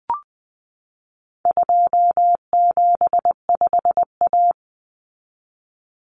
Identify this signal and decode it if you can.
Morse code, heard as a keyed tone, about 20 wpm, 700 Hz. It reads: 275A
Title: Common Cut Numbers - 4-Digit 20wpm